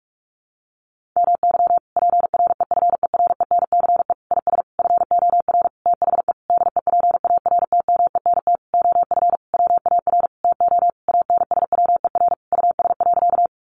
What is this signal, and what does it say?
Morse code, heard as a keyed tone, about 44 wpm, 700 Hz. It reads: MY PREFERENCE IS FOR THE DEPARTMENT OF WAR TO ANSWER US.
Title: Sentences from Top 1500 Words 45wpm